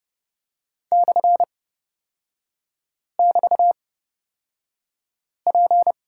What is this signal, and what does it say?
Morse code, heard as a keyed tone, about 30 wpm, spaced as for 5 wpm, 700 Hz. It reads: /<BT>P
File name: Lesson 36e - BT - Three Characters 30wpm